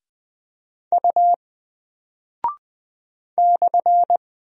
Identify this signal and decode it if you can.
Morse code, heard as a keyed tone, about 20 wpm, 700 Hz. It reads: U /
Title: Single Character (Letter:Number:Punctuation:Prosign) Alt 20wpm